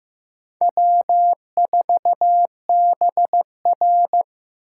Morse code, heard as a keyed tone, about 18 wpm, 700 Hz. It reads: W4BR
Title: Speed Racing - USA 1x2 Call Signs 15wpm